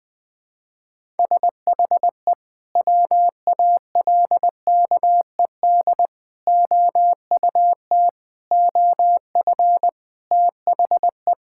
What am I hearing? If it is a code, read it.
Morse code, heard as a keyed tone, about 21 wpm, 700 Hz. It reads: SHE WALKED OUT OF THE
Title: Sentences with 3-Word Phrases - 20wpm